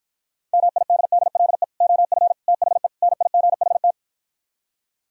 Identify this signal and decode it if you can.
Morse code, heard as a keyed tone, about 52 wpm, 700 Hz. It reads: MIDDLE OF THE NIGHT